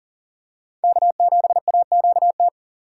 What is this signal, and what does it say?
Morse code, heard as a keyed tone, about 40 wpm, 700 Hz. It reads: K7AQT
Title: USA Call Signs - 5 Characters - 6x Character Spacing - 40wpm